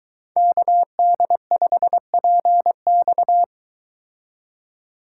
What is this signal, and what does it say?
Morse code, heard as a keyed tone, about 22 wpm, 700 Hz. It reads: KD5PX